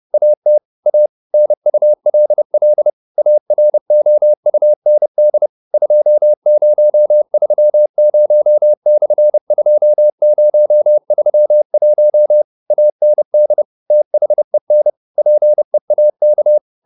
Morse code, heard as a keyed tone, about 30 wpm, 600 Hz. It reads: AT A NULL AROUND 2030/2031 AND THEN PEAK